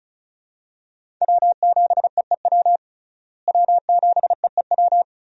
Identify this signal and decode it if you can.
Morse code, heard as a keyed tone, about 36 wpm, 700 Hz. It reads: W7EEW W7EEW
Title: USA Call Signs - 5 Characters - Repeated 2x - 35wpm